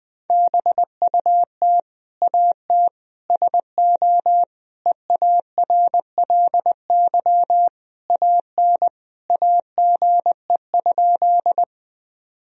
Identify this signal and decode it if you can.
Morse code, heard as a keyed tone, about 21 wpm, 700 Hz. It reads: BUT AT SO EARLY AN AGE?
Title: Sentences from Top 1000 Words - Review Part 3 - 20wpm